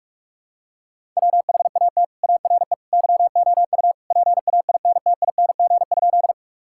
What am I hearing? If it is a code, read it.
Morse code, heard as a keyed tone, about 43 wpm, 700 Hz. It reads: WHAT ARE YOU PAINTING?
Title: Sentences from Top 1000 Words - Review Part 6 - 45wpm